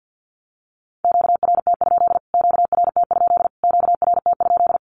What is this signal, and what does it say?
Morse code, heard as a keyed tone, about 49 wpm, 700 Hz. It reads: QRT? QRT? QRT?
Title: Q-Codes Repeated 3x 50wpm